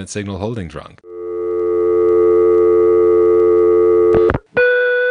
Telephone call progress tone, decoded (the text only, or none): dial tone